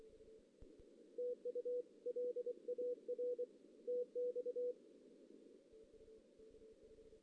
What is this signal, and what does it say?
Morse code, heard as a keyed tone, about 24 wpm, 490 Hz. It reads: TU L AR TX
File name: 2016 JAN CW NAQP / 2016 Jan naqp k5ot - 0-3